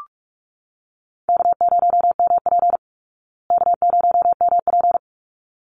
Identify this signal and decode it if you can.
Morse code, heard as a keyed tone, about 44 wpm, 700 Hz. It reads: K0MP K0MP
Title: Call Signs Repeated 3x - 45wpm